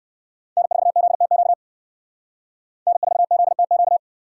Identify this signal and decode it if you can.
Morse code, heard as a keyed tone, about 69 wpm, 700 Hz. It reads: N4ZTQ N4ZTQ